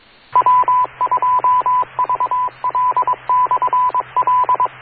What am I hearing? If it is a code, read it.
Morse code, heard as a keyed tone, about 22 wpm, 1000 Hz, with radio noise. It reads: W2VL/L